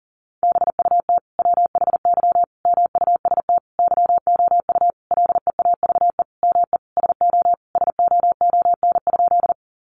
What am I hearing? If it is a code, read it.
Morse code, heard as a keyed tone, about 40 wpm, 700 Hz. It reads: BUT WHY MUST YOU LEAVE ME SO SOON?
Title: Sentences from Top 400 Words 40wpm